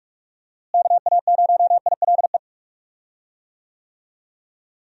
Morse code, heard as a keyed tone, about 44 wpm, 700 Hz. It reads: KA0ILE
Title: USA Call Signs - 6 Characters - Repeated 2x - 45wpm